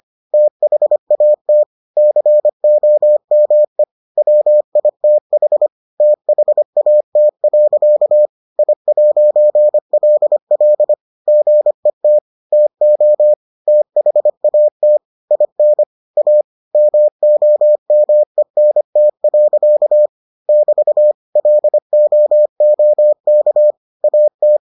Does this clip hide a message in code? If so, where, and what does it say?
Yes: Morse code from 0.3 to 24.6 s, heard as a keyed tone, about 25 wpm, 600 Hz. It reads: THAT COME WITH THAT. I'LL GET TO THAT IN A MOMENT. <BT> LOOK AT